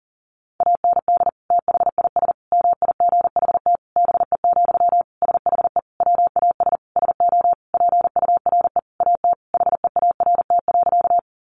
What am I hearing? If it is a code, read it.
Morse code, heard as a keyed tone, about 40 wpm, 700 Hz. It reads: AND THIS MIGHT BE, SHE WAS SO PURE AT HEART.
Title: Sentences from Top 1500 Words 40wpm